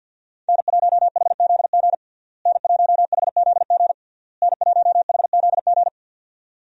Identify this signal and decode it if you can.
Morse code, heard as a keyed tone, about 51 wpm, 700 Hz. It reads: N1HZG N1HZG N1HZG
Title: USA Call Signs - 5 Characters - Repeated 3x - 50wpm